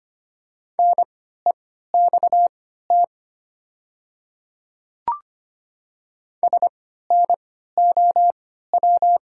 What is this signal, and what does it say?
Morse code, heard as a keyed tone, about 25 wpm, spaced as for 14 wpm, 700 Hz. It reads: NEXT SNOW
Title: 3x Character Spacing - Top 500 Words - 25wpm